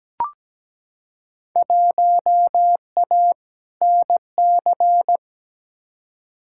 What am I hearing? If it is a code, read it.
Morse code, heard as a keyed tone, about 17 wpm, 700 Hz. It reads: 1A NC
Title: Field Day Exchanges Encore - 17wpm